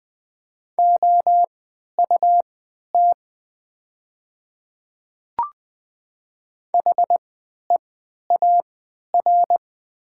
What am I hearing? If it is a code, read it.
Morse code, heard as a keyed tone, about 20 wpm, spaced as for 11 wpm, 700 Hz. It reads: OUT HEAR
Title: 3x Character Spacing - Top 300 Words - 20wpm